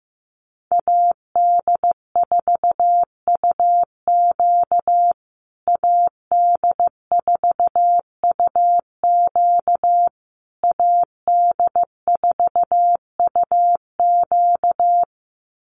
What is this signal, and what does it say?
Morse code, heard as a keyed tone, about 15 wpm, 700 Hz. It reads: AD4UQ AD4UQ AD4UQ
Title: USA Call Signs Repeated 3x 15wpm